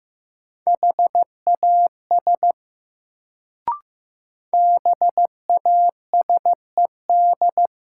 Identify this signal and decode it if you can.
Morse code, heard as a keyed tone, about 15 wpm, 700 Hz. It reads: HAS BASED
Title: Sets of 2 Words - New Word Drill - 15wpm